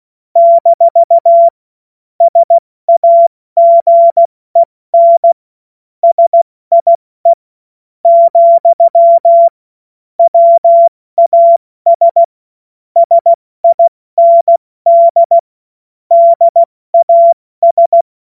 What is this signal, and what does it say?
Morse code, heard as a keyed tone, about 16 wpm, spaced as for 13 wpm, 680 Hz. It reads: - SAGEN SIE, WAS SIND DAS
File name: Morsekurs / Witze / Witz 20 Februar 2024 70 F80